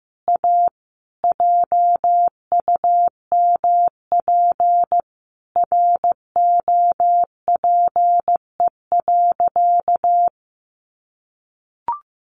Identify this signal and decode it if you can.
Morse code, heard as a keyed tone, about 15 wpm, 700 Hz. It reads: A JUMP ROPE.
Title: Easy Phrases 15wpm